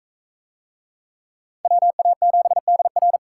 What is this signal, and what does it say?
Morse code, heard as a keyed tone, about 40 wpm, 700 Hz. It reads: WA7DR